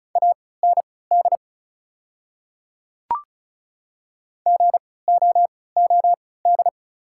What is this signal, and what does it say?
Morse code, heard as a keyed tone, about 34 wpm, spaced as for 20 wpm, 700 Hz. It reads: AND GOOD
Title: 3x Character Spacing - Top 300 Words - 35wpm